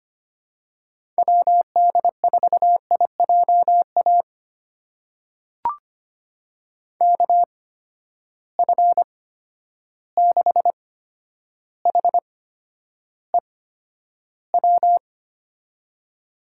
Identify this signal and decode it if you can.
Morse code, heard as a keyed tone, about 25 wpm, spaced as for 11 wpm, 700 Hz. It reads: WD4IJA KF6HEW
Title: USA Call Signs - 6 Characters - 8x Character Spacing - 25wpm